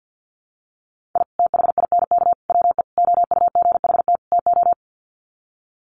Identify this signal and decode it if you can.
Morse code, heard as a keyed tone, about 49 wpm, 700 Hz. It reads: I THINK WE OUGHT TO
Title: Sets of 5 Words Encore 50wpm